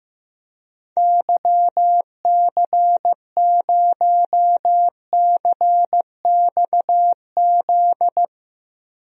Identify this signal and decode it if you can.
Morse code, heard as a keyed tone, about 15 wpm, 700 Hz. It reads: YC0CXZ
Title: DXCC-100 Call Signs - 6 Characters - Repeated 2x - 15wpm